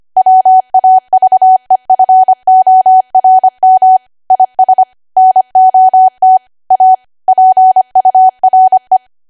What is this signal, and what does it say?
Morse code, heard as a keyed tone, about 25 wpm, 750 Hz. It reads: WAVEFORM IS NOT A PURE
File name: cwa / W1AW-Level3 / 25WPM